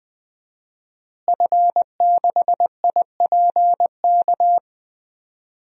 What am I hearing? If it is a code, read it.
Morse code, heard as a keyed tone, about 20 wpm, 700 Hz. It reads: F6IPK
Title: DXCC-100 Call Signs - 5 Characters - Repeated 2x - 20wpm